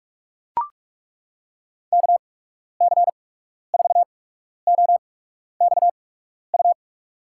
Morse code, heard as a keyed tone, about 42 wpm, spaced as for 12 wpm, 700 Hz. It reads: KC4OXU
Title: USA Call Signs - 6 Characters - 8x Character Spacing - 45wpm